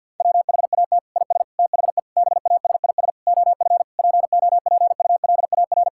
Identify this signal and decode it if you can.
Morse code, heard as a keyed tone, about 51 wpm, 700 Hz. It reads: WHAT IS THE BASIS OF POPULAR
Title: Sentences from Top 1000 Words - Review Part 5 - 50wpm